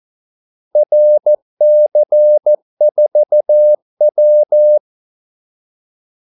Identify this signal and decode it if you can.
Morse code, heard as a keyed tone, about 14 wpm, 600 Hz. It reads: RC4W
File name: Callsigns-14wpm